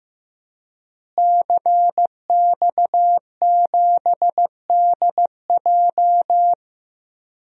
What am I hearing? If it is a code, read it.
Morse code, heard as a keyed tone, about 15 wpm, 700 Hz. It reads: CX7DJ
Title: DXCC-100 Call Signs - 5 Characters - Repeated 3x - 15wpm